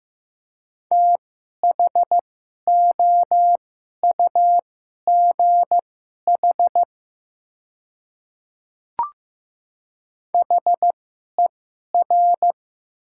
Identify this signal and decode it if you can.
Morse code, heard as a keyed tone, about 15 wpm, spaced as for 11 wpm, 700 Hz. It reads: THOUGH HER
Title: 2x Character Spacing - Top 500 Words - 15wpm